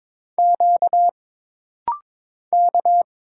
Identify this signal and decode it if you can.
Morse code, heard as a keyed tone, about 23 wpm, 700 Hz. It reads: Q K
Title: Single Letters - Rapid-Fire 22wpm